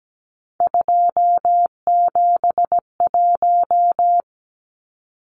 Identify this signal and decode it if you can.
Morse code, heard as a keyed tone, about 17 wpm, 700 Hz. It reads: 271
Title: Numbers - 3-Digit 17wpm